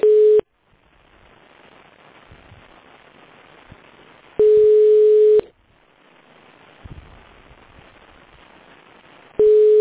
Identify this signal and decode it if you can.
Ringback tone (Europe, 425 Hz).